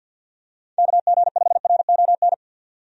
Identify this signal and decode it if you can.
Morse code, heard as a keyed tone, about 47 wpm, 700 Hz. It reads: KG5RON
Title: USA Call Signs - 6 Characters - 50wpm